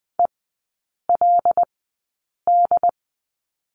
Morse code, heard as a keyed tone, about 20 wpm, spaced as for 8 wpm, 700 Hz. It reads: ELD
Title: 093 Lesson 16e - Letter c - Three Characters 20wpm